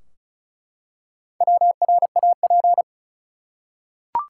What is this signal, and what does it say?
Morse code, heard as a keyed tone, about 36 wpm, 700 Hz. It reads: WRAP
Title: Four Letter Words - 35wpm